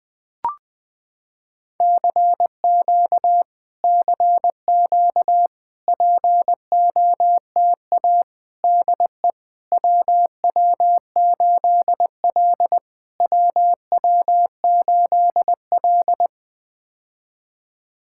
Morse code, heard as a keyed tone, about 21 wpm, 700 Hz. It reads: CQ CQ POTA DE WW8L WW8L
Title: POTA QSO Practice - 20wpm